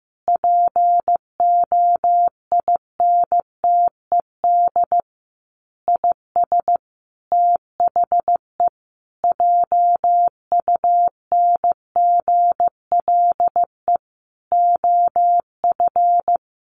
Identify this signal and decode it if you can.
Morse code, heard as a keyed tone, about 15 wpm, 700 Hz. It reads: POINTED IS THE JUNGLE OF
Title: A Sound Of Thunder 15wpm